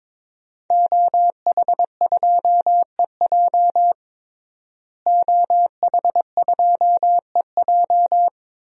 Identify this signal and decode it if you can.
Morse code, heard as a keyed tone, about 22 wpm, 700 Hz. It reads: OH2EJ OH2EJ
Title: DXCC-100 Call Signs - 5 Characters - Repeated 3x - 22wpm